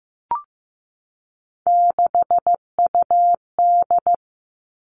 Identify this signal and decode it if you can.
Morse code, heard as a keyed tone, about 15 wpm, 700 Hz. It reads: 6UD
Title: Three Challenging Letter-Number Combinations 15wpm